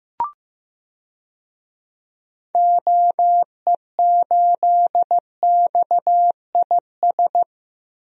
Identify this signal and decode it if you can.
Morse code, heard as a keyed tone, about 15 wpm, 700 Hz. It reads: OE8XIS
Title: DXCC-100 Call Signs - 6 Characters - Repeated 3x - 15wpm